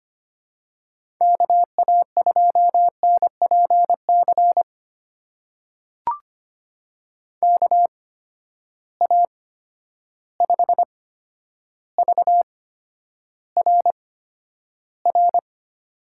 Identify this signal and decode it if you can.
Morse code, heard as a keyed tone, about 25 wpm, spaced as for 11 wpm, 700 Hz. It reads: KA2NPC KA5VRR
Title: USA Call Signs - 6 Characters - 8x Character Spacing - 25wpm